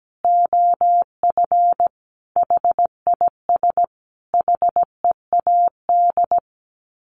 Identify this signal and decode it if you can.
Morse code, heard as a keyed tone, about 17 wpm, 700 Hz. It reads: OF HIS HEAD
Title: Sets of 5 Words Encore 17wpm